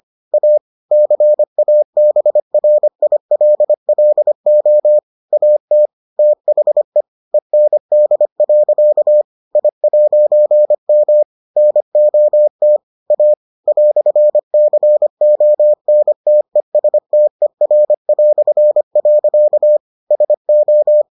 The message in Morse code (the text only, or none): A CABRILLO AT THE END. I'M NOT A "CONTESTER". SO